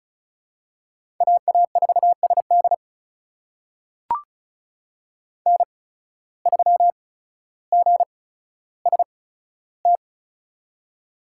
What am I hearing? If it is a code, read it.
Morse code, heard as a keyed tone, about 36 wpm, spaced as for 15 wpm, 700 Hz. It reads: AA4SD N3GST
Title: USA Call Signs - 5 Characters - 8x Character Spacing - 35wpm